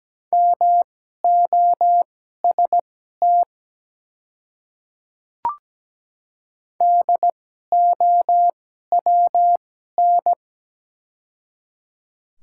Morse code, heard as a keyed tone, about 17 wpm, spaced as for 12 wpm, 700 Hz. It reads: MOST DOWN
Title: 2x Character Spacing - Top 100 Words - 17wpm